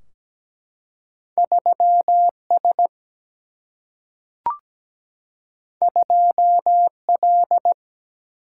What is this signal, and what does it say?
Morse code, heard as a keyed tone, about 17 wpm, 700 Hz. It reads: 3 S 2 L
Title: Single-Letter Single-Number 17wpm